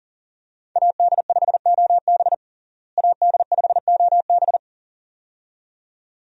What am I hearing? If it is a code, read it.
Morse code, heard as a keyed tone, about 40 wpm, 700 Hz. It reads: AD5OB AD5OB
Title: USA Call Signs - 5 Characters - Repeated 2x - 40wpm